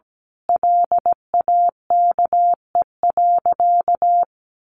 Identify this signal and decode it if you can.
Morse code, heard as a keyed tone, about 17 wpm, 700 Hz. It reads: LAKE.
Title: Sentences from Top 1100 Words 17wpm